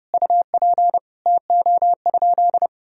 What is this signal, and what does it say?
Morse code, heard as a keyed tone, about 32 wpm, 700 Hz. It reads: UP TO?
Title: Sentences from Top 100 Words and Abbreviations 30wpm